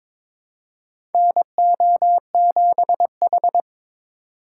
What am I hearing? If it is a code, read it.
Morse code, heard as a keyed tone, about 22 wpm, 700 Hz. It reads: NO7H
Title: USA Call Signs - 2x Character Spacing - 22wpm